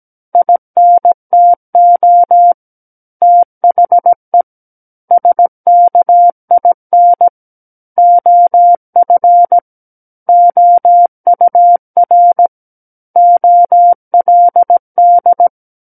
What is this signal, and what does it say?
Morse code, heard as a keyed tone, about 17 wpm, 700 Hz. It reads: INTO THE SKIN OF OUR OLD
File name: Book One / 08-Chapter 8 - 17 WPM